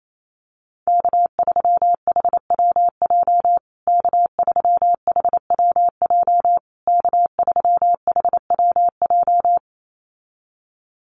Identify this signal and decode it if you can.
Morse code, heard as a keyed tone, about 28 wpm, 700 Hz. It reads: K3HWJ K3HWJ K3HWJ
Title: USA Call Signs Repeated 3x 28wpm